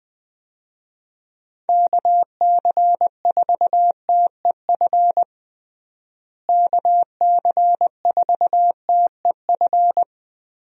Morse code, heard as a keyed tone, about 20 wpm, 700 Hz. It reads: KC4TEF KC4TEF
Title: USA Call Signs - 6 Characters - Repeated 2x - 20wpm